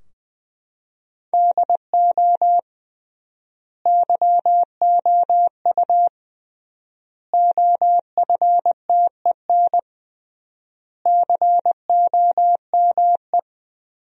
Morse code, heard as a keyed tone, about 21 wpm, 700 Hz. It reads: DO YOU OFTEN COME
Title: 3x Word-Spacing - Sentences from Top 300 Words - 20wpm